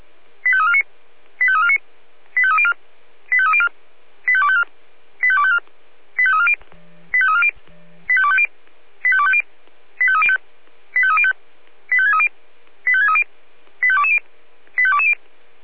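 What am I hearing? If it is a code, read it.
Five-tone selective call (ZVEI), heard as quick runs of five tones, tones of about 70 ms. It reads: 85349 85349 85304 85304 85254 85254 85340 85340 85250 85250 85305 85305 85630 85630 85220 85220
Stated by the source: ZVEI five-tone selective call, code 85888, received by SDR.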